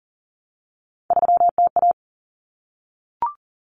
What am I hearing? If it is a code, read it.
Morse code, heard as a keyed tone, about 40 wpm, 700 Hz. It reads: 3TA
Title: Common Cut Numbers - 3-Digit 40wpm